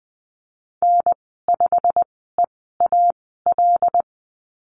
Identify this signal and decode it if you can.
Morse code, heard as a keyed tone, about 20 wpm, spaced as for 15 wpm, 700 Hz. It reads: N5EAL
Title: 088 Lesson 15g - Number 5 - Call Signs 20wpm